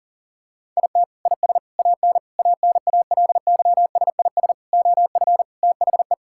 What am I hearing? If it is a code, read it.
Morse code, heard as a keyed tone, about 40 wpm, 700 Hz. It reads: IT IS AN ANALYSIS OF THE
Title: Sentences from Top 1000 Words - Review Part 5 - 40wpm